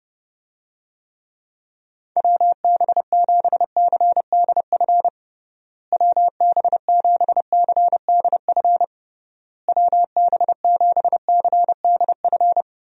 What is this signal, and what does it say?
Morse code, heard as a keyed tone, about 30 wpm, 700 Hz. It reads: WB7CDF WB7CDF WB7CDF
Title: USA Call Signs - 6 Characters - Repeated 3x - 30wpm